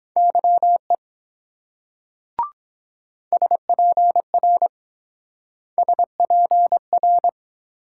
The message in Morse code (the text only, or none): YE SPR SPR